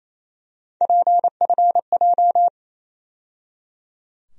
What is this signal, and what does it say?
Morse code, heard as a keyed tone, about 28 wpm, 700 Hz. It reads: PFJ
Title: USA Call Sign Postfixes - 3 Letters 28wpm